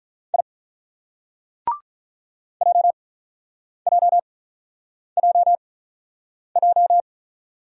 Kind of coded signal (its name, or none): Morse code